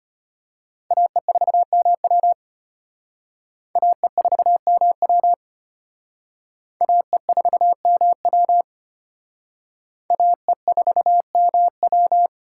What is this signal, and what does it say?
Morse code, heard as a keyed tone, about 32 wpm, 700 Hz. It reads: AE4MW AE4MW AE4MW AE4MW